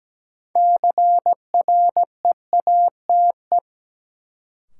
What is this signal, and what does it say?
Morse code, heard as a keyed tone, about 17 wpm, 700 Hz. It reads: CREATE
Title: Sentences from Top 1600 Words - New Word Drill Alt - 17wpm